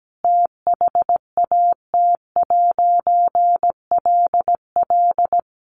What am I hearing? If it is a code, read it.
Morse code, heard as a keyed tone, about 17 wpm, 700 Hz. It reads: THAT'LL
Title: A Sound Of Thunder 17wpm